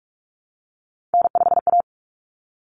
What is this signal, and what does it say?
Morse code, heard as a keyed tone, about 44 wpm, 700 Hz. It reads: N5A
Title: Common Cut Numbers - 3-Digit 45wpm